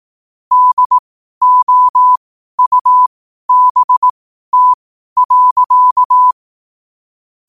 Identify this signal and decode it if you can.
Morse code, heard as a keyed tone, about 18 wpm, spaced as for 12 wpm, 1000 Hz. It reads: DOUBT.
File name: qotdfiles / 12wpm